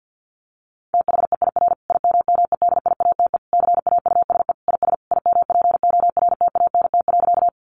Morse code, heard as a keyed tone, about 50 wpm, 700 Hz. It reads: THEIR IMMEDIATE CAUSE IS IMPORTANT.
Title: Sentences from Top 1200 Words 50wpm